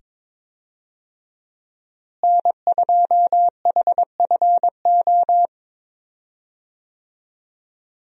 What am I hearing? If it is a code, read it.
Morse code, heard as a keyed tone, about 22 wpm, 700 Hz. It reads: N2HFO
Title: USA Call Signs - 5 Characters - Repeated 2x - 22wpm